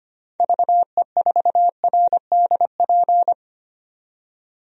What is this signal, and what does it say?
Morse code, heard as a keyed tone, about 25 wpm, 700 Hz. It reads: VE4RDP
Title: DXCC-100 Call Signs - 6 Characters - 25wpm